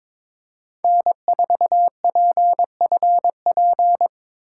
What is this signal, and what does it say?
Morse code, heard as a keyed tone, about 26 wpm, 700 Hz. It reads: N4PFP
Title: Speed Racing - USA Call Signs 22wpm